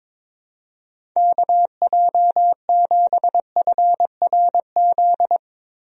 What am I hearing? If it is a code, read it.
Morse code, heard as a keyed tone, about 22 wpm, 700 Hz. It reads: KJ7FRZ
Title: USA Call Signs - 6 Characters - Repeated 3x - 22wpm